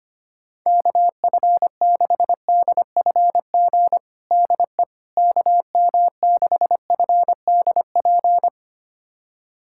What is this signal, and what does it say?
Morse code, heard as a keyed tone, about 25 wpm, 700 Hz. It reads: KF6DFG DE KM6FDP
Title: USA Callsign DE Callsign - 6 Characters - 25wpm